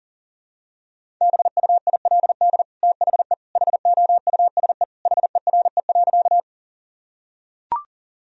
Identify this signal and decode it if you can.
Morse code, heard as a keyed tone, about 40 wpm, 700 Hz. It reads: BUILD THE HOUSE HERE.
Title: Sentences from Top 1000 Words - Review Part 1 - 40wpm